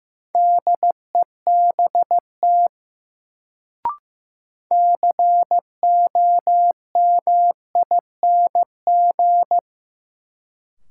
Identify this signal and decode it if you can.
Morse code, heard as a keyed tone, about 15 wpm, 700 Hz. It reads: DEBT COMING
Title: Sentences from Top 1300 Words - New Word Drill Alt - 15wpm